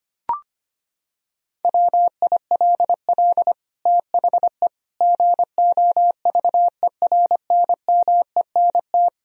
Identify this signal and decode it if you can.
Morse code, heard as a keyed tone, about 25 wpm, 700 Hz. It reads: WILL THE GOVERNMENT
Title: Sentences from Top 1000 Words - Review Part 3 - 25wpm